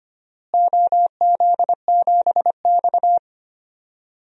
Morse code, heard as a keyed tone, about 25 wpm, 700 Hz. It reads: OZ7X
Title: DXCC-100 Call Signs - 4 Characters - Repeated 3x - 25wpm